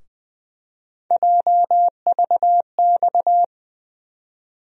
Morse code, heard as a keyed tone, about 21 wpm, 700 Hz. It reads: JVX
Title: USA Call Sign Postfixes - 3 Letters 20wpm